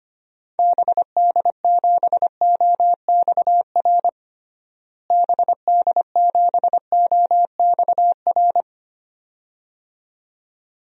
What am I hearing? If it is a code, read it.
Morse code, heard as a keyed tone, about 25 wpm, 700 Hz. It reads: BD7OXR BD7OXR
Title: DXCC-100 Call Signs - 6 Characters - Repeated 2x - 25wpm